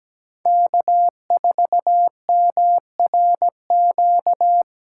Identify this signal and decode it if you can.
Morse code, heard as a keyed tone, about 20 wpm, 700 Hz. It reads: K4MRQ
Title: Speed Racing - USA 1x3 Call Signs 17wpm